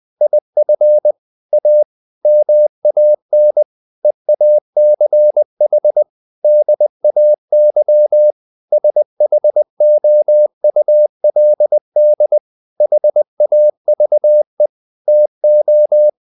Morse code, heard as a keyed tone, about 20 wpm, 600 Hz. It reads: IF A MAN EACH DAY SHOULD HAVE TO
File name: the old man and the sea part 2 20wpm